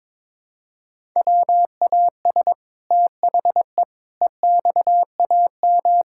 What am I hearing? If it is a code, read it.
Morse code, heard as a keyed tone, about 22 wpm, 700 Hz. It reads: WAS THE EXAM
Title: Sentences from Top 1000 Words - Review Part 5 - 22wpm